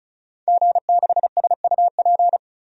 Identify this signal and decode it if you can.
Morse code, heard as a keyed tone, about 36 wpm, 700 Hz. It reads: G6SUP
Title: DXCC-100 Call Signs - 5 Characters - 35wpm